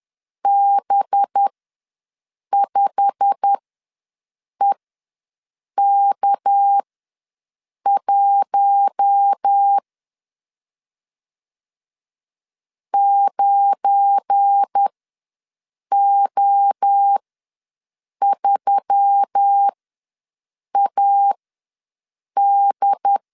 Morse code, heard as a keyed tone, about 10 wpm, spaced as for 6 wpm, 800 Hz. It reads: B5EK1 9O3AD
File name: LEZIONI / 8 YJ45 / 8 YJ45 ws 10wpm 3843